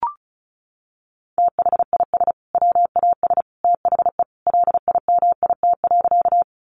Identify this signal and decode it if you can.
Morse code, heard as a keyed tone, about 34 wpm, 700 Hz. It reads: THIS WAS THE LIMIT.
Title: Sentences from Top 1500 Words 35wpm